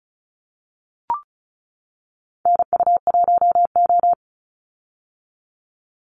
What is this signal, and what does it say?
Morse code, heard as a keyed tone, about 34 wpm, 700 Hz. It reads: NU1O
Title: Call Signs - 35wpm